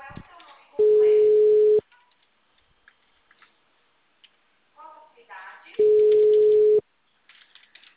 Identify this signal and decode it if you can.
Ringback tone (Europe, 425 Hz), with speech.